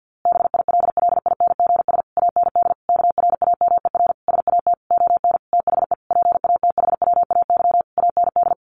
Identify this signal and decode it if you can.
Morse code, heard as a keyed tone, about 49 wpm, 700 Hz. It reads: BILLINGS AND KRAMER SAT ON THE PATHWAY AND
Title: A Sound Of Thunder 50wpm